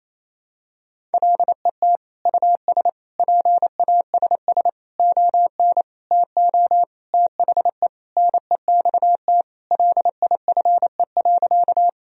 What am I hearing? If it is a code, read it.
Morse code, heard as a keyed tone, about 27 wpm, 700 Hz. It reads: LET US PASS ON TO THE NEXT LIFE.
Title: Sentences from Top 1000 Words - Review Part 1 - 28wpm